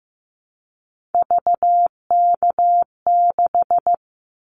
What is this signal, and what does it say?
Morse code, heard as a keyed tone, about 15 wpm, 700 Hz. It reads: VK6
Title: Callsign Prefixes for Top 100 DXCC - 15wpm